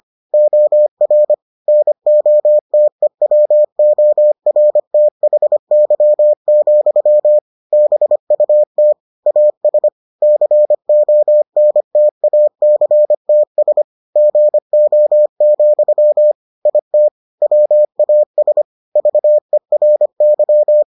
Morse code, heard as a keyed tone, about 25 wpm, 600 Hz. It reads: OR NOTEWORTHY, BUT AS CONTACTS GO, IT WAS VERY